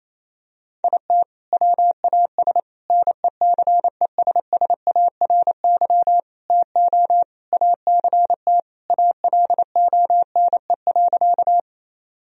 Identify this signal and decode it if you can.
Morse code, heard as a keyed tone, about 28 wpm, 700 Hz. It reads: IT WAS NECESSARY TO ACT ALONE.
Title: Sentences from Top 1000 Words - Review Part 1 - 28wpm